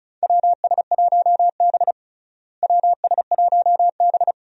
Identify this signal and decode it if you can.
Morse code, heard as a keyed tone, about 36 wpm, 700 Hz. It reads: WS1B WS1B
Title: USA Call Signs - 4 Characters - Repeated 2x - 35wpm